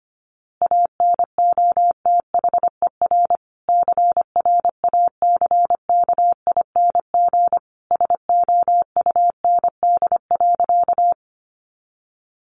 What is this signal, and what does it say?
Morse code, heard as a keyed tone, about 25 wpm, 700 Hz. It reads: ANOTHER CRACKING SOUND.
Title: A Sound Of Thunder 25wpm